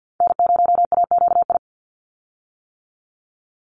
Morse code, heard as a keyed tone, about 50 wpm, 700 Hz. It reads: N0AQI
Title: USA Call Signs 50wpm